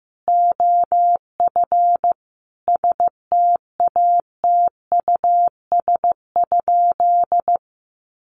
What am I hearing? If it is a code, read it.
Morse code, heard as a keyed tone, about 15 wpm, 700 Hz. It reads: OF STATUS?
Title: Sentences from Top 1400 Words 15wpm